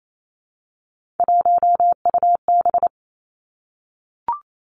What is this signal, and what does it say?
Morse code, heard as a keyed tone, about 28 wpm, 700 Hz. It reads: 1UB
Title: Three Challenging Letter-Number Combinations 28wpm